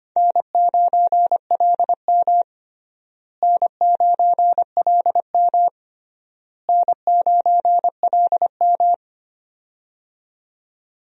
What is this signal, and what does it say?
Morse code, heard as a keyed tone, about 25 wpm, 700 Hz. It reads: N9LM N9LM N9LM
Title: USA Call Signs - 4 Characters - Repeated 3x - 25wpm